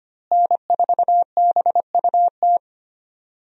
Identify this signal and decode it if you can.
Morse code, heard as a keyed tone, about 25 wpm, 700 Hz. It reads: N4BUT
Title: USA Call Signs - 5 Characters - Repeated 3x - 25wpm